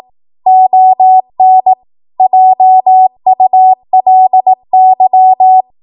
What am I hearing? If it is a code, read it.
Morse code, heard as a keyed tone, about 18 wpm, 750 Hz. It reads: ON JULY